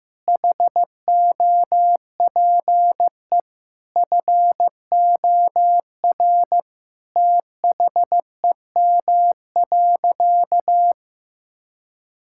Morse code, heard as a keyed tone, about 16 wpm, 700 Hz. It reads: HOPE FOR THEM.
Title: Sentences from Top 1000 Words - Review Part 5 - 15wpm